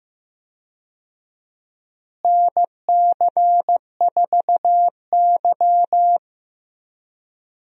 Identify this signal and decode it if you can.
Morse code, heard as a keyed tone, about 15 wpm, 700 Hz. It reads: NC4Y
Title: USA Call Signs - 4 Characters - Repeated 2x - 15wpm